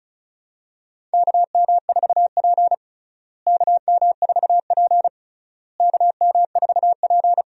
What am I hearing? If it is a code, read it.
Morse code, heard as a keyed tone, about 36 wpm, 700 Hz. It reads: KM4P KM4P KM4P
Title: USA Call Signs - 4 Characters - Repeated 3x - 35wpm